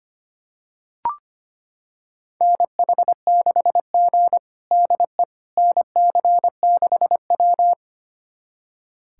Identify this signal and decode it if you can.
Morse code, heard as a keyed tone, about 25 wpm, 700 Hz. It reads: NH6G DE NC6W
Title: USA Callsign DE Callsign - 4 Characters - 25wpm